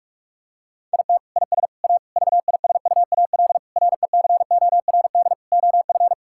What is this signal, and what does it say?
Morse code, heard as a keyed tone, about 41 wpm, 700 Hz. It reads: IT IS A VISUAL RECORD OF
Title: Sentences from Top 1000 Words - Review Part 6 - 45wpm